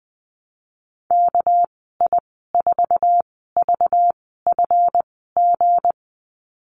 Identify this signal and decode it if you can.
Morse code, heard as a keyed tone, about 20 wpm, spaced as for 14 wpm, 700 Hz. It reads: KI4VFG
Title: 245 Lesson 39g - Letter z - Call Signs 20wpm